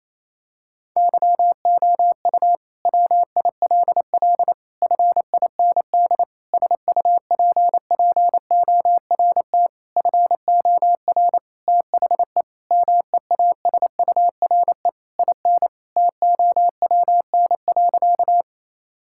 Morse code, heard as a keyed tone, about 28 wpm, 700 Hz. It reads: YOU WILL FIND SUPPORT FOR THE MEASURE IN TOWN.
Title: Sentences from Top 1000 Words - Review Part 5 - 28wpm